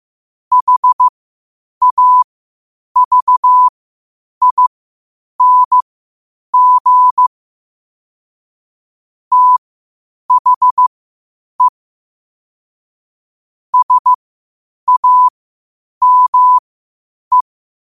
Morse code, heard as a keyed tone, about 15 wpm, spaced as for 8 wpm, 1000 Hz. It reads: HAVING THE SAME
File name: qotdfiles / 8wpm